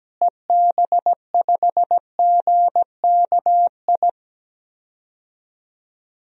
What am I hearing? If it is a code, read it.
Morse code, heard as a keyed tone, about 17 wpm, 700 Hz. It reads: EB5GKI
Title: DXCC-100 Call Signs - 6 Characters - Repeated 2x - 17wpm